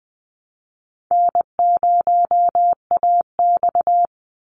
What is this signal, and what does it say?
Morse code, heard as a keyed tone, about 20 wpm, 700 Hz. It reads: N0AX
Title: Easy Call Signs Repeated 3x - 20wpm